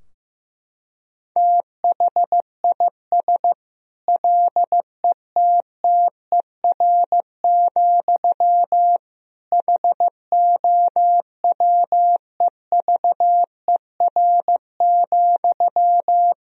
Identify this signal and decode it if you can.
Morse code, heard as a keyed tone, about 15 wpm, 700 Hz. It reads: THIS LETTER, HOWEVER,
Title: Sentences from Top 1000 Words - Review Part 4 - 15wpm